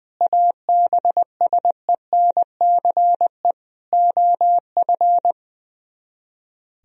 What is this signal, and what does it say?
Morse code, heard as a keyed tone, about 20 wpm, 700 Hz. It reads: ABSENCE OF
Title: Speed-Racing - Sets of 4 Words 20wpm